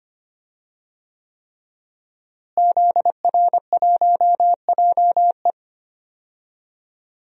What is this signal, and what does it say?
Morse code, heard as a keyed tone, about 25 wpm, 700 Hz. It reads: ZR1JE
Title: DXCC-100 Call Signs - 5 Characters - Repeated 3x - 25wpm